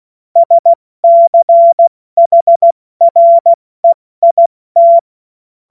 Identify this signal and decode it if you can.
Morse code, heard as a keyed tone, about 16 wpm, 680 Hz. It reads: SCHREIT
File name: Morsekurs / Witze / Witz 17. August 2021, 70 F80